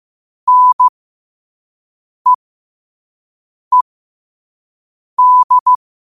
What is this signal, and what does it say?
Morse code, heard as a keyed tone, about 15 wpm, spaced as for 5 wpm, 1000 Hz. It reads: NEED
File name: qotdfiles / 5wpm